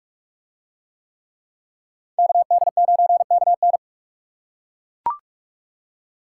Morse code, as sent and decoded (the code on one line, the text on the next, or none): -.- -.. ----. -.- -.
KD9KN